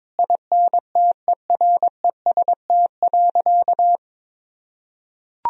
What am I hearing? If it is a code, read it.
Morse code, heard as a keyed tone, about 22 wpm, 700 Hz. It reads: INTEREST.
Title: Sentences from Top 1000 Words - Review Part 4 - 22wpm